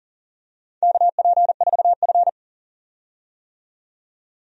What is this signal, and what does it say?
Morse code, heard as a keyed tone, about 40 wpm, 700 Hz. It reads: KP4F
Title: DXCC-100 Call Signs - 4 Characters - 40wpm